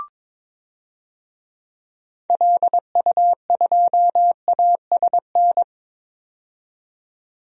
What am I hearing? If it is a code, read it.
Morse code, heard as a keyed tone, about 22 wpm, 700 Hz. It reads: LU2ASN
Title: DXCC-100 Call Signs - 6 Characters - 22wpm